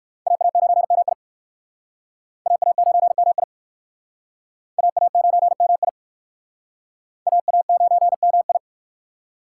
Morse code, heard as a keyed tone, about 60 wpm, 700 Hz. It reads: AA9MI AA9MI AA9MI AA9MI